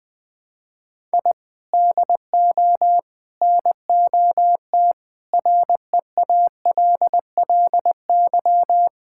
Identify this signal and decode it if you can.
Morse code, heard as a keyed tone, about 20 wpm, 700 Hz. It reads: I DO NOT REALLY
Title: Sentences with 4-Word Phrases - 20wpm